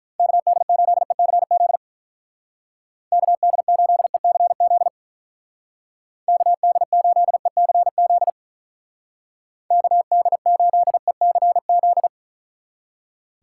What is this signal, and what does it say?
Morse code, heard as a keyed tone, about 44 wpm, 700 Hz. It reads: KD8ECZ KD8ECZ KD8ECZ KD8ECZ